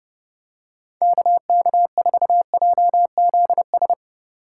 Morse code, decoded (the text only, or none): KK4JZS